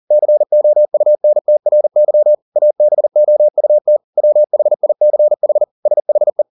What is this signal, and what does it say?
Morse code, heard as a keyed tone, about 40 wpm, 600 Hz. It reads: COUNTRY ABOUT WHICH SHE